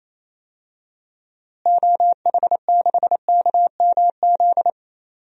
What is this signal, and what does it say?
Morse code, heard as a keyed tone, about 27 wpm, 700 Hz. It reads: OH6KMZ
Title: DXCC-100 Call Signs - 6 Characters - Repeated 3x - 28wpm